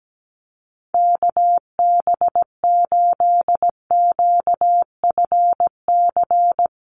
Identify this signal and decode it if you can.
Morse code, heard as a keyed tone, about 17 wpm, 700 Hz. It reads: KB8QFC
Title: USA Call Signs Repeated 2x 17wpm